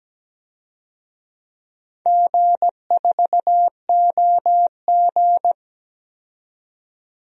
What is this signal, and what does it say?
Morse code, heard as a keyed tone, about 17 wpm, 700 Hz. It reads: G4OG
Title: DXCC-100 Call Signs - 4 Characters - 17wpm